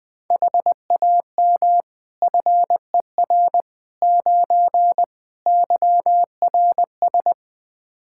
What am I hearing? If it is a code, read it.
Morse code, heard as a keyed tone, about 20 wpm, 700 Hz. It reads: HAM FER 9 YRS
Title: K1USN SST QSO Practice 20wpm